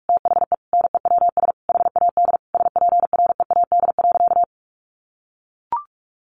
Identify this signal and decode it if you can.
Morse code, heard as a keyed tone, about 44 wpm, 700 Hz. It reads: THE NEWS HAD SPREAD.
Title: Sentences from Top 1200 Words 45wpm